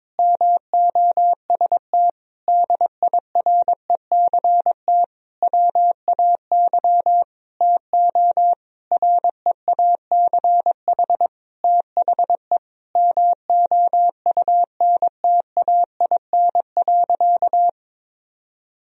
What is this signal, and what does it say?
Morse code, heard as a keyed tone, about 22 wpm, 700 Hz. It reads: MOST DIRECT WAY TO REACH THE MOUNTAIN.
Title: Sentences from Top 1000 Words - Review Part 2 - 22wpm